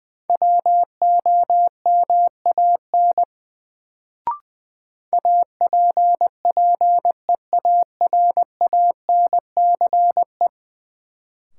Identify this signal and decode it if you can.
Morse code, heard as a keyed tone, about 20 wpm, 700 Hz. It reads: WOMAN APPEARANCE
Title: Sentences from Top 1100 Words - New Word Drill Alt - 20wpm